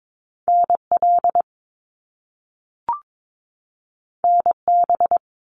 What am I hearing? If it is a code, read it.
Morse code, heard as a keyed tone, about 22 wpm, 700 Hz. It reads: NL NB
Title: Canadian Provinces and Territories 22wpm